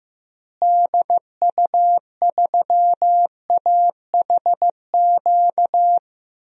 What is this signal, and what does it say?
Morse code, heard as a keyed tone, about 15 wpm, 700 Hz. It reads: DU3AHQ
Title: DXCC-100 Call Signs - 6 Characters - Repeated 2x - 15wpm